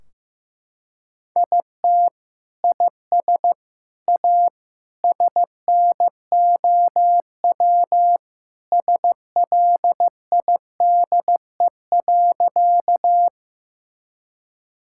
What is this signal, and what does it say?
Morse code, heard as a keyed tone, about 15 wpm, 700 Hz. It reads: IT IS A SNOW SLIDE.
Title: Sentences from Top 1600 Words 15wpm